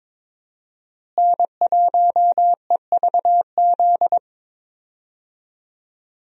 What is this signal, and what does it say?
Morse code, heard as a keyed tone, about 21 wpm, 700 Hz. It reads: N1EVZ
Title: USA Call Signs - 5 Characters - Repeated 3x - 22wpm